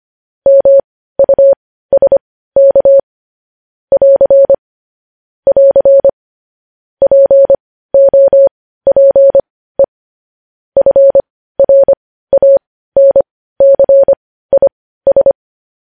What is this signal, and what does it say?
Morse code, heard as a keyed tone, about 25 wpm, spaced as for 15 wpm, 550 Hz. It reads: MUSK <AR> <AR> POPE FRANCIS
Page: News Headlines in Morse Code at 15 WPM